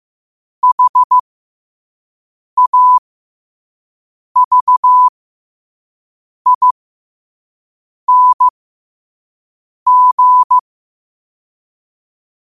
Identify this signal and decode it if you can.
Morse code, heard as a keyed tone, about 15 wpm, spaced as for 5 wpm, 1000 Hz. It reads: HAVING
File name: qotdfiles / 5wpm